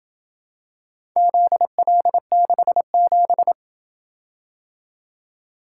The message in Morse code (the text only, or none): ZL67